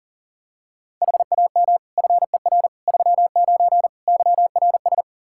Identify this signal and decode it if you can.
Morse code, heard as a keyed tone, about 37 wpm, 700 Hz. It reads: HAM FER 39 YRS
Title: Practice QSO - Exchange 3 - More Info - 40wpm